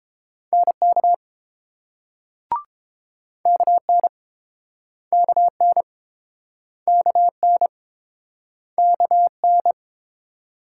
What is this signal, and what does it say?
Morse code, heard as a keyed tone, about 30 wpm, 700 Hz. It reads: NK KN KN KN KN